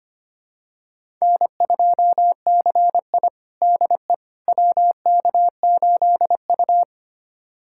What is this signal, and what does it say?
Morse code, heard as a keyed tone, about 25 wpm, 700 Hz. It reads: N2CI DE WK8U
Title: USA Callsign DE Callsign - 4 Characters - 25wpm